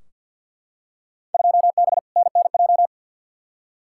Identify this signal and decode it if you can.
Morse code, heard as a keyed tone, about 53 wpm, 700 Hz. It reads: 2B NNJ
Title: Field Day Exchanges Encore - 50wpm